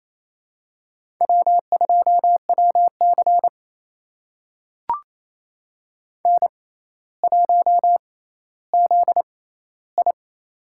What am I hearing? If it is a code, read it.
Morse code, heard as a keyed tone, about 28 wpm, spaced as for 14 wpm, 700 Hz. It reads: W2WC N1ZI
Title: USA Call Signs - 6x Character Spacing - 28wpm